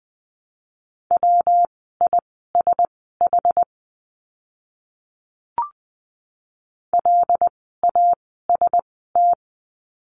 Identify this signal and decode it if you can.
Morse code, heard as a keyed tone, about 20 wpm, spaced as for 14 wpm, 700 Hz. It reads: WISH LAST
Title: 115 Lesson 19f - Letter w - Words 20wpm